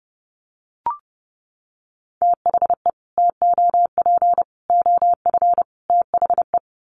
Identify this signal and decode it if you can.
Morse code, heard as a keyed tone, about 30 wpm, 700 Hz. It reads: THE TOP OF THE
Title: Sets of 4 Words 30wpm